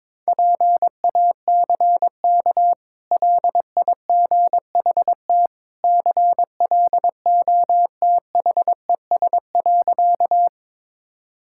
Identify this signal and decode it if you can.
Morse code, heard as a keyed tone, about 22 wpm, 690 Hz. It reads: PACK LIGHT CLOTHES.
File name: practice / 22wpm / Easy Sentences with Hints - 22wpm